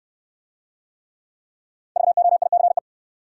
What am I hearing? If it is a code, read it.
Morse code, heard as a keyed tone, about 59 wpm, 700 Hz. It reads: VOICE